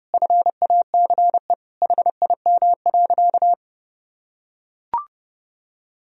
Morse code, heard as a keyed tone, about 30 wpm, 700 Hz. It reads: FACE HIM.
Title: Sentences from Top 1000 Words - Review Part 2 - 30wpm